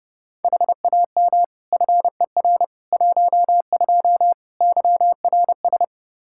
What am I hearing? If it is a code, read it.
Morse code, heard as a keyed tone, about 30 wpm, 700 Hz. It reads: HAM FER 12 YRS
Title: Practice QSO - Exchange 3 - More Info - 30wpm